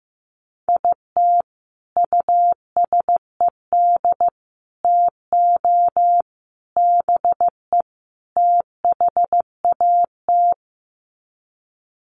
Morse code, heard as a keyed tone, about 15 wpm, 700 Hz. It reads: IT USED TO BE THAT
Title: Sets of 5 Words Encore2 15wpm